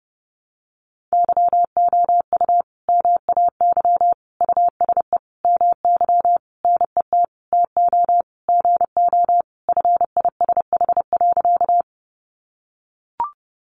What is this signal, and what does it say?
Morse code, heard as a keyed tone, about 30 wpm, 700 Hz. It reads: YOU MAY USE MY NET TO GO FISH.
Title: Sentences from Top 1400 Words 30wpm